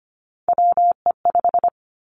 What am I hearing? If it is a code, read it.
Morse code, heard as a keyed tone, about 25 wpm, 700 Hz. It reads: WE5
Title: Three Challenging Letter-Number Combinations 25wpm